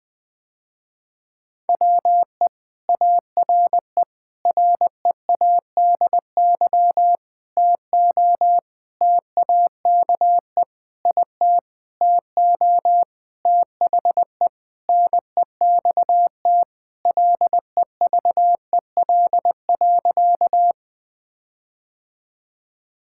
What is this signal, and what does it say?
Morse code, heard as a keyed tone, about 20 wpm, 700 Hz. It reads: WE ARE READY TO TAKE IT TO THE NEXT LEVEL.
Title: Sentences with 7-Word Phrases - Spoken Beginning - 20wpm